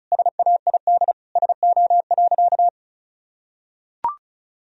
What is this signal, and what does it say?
Morse code, heard as a keyed tone, about 36 wpm, 700 Hz. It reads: SAID SO.
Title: Sentences from Top 1000 Words - Review Part 4 - 35wpm